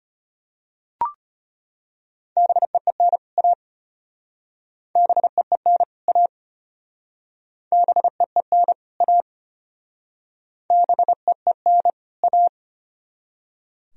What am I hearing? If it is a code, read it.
Morse code, heard as a keyed tone, about 32 wpm, 700 Hz. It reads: BEEN A BEEN A BEEN A BEEN A